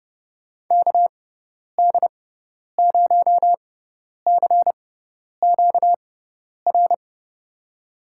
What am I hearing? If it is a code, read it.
Morse code, heard as a keyed tone, about 30 wpm, spaced as for 10 wpm, 700 Hz. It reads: KD0CQR
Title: USA Call Signs - 6x Character Spacing - 30wpm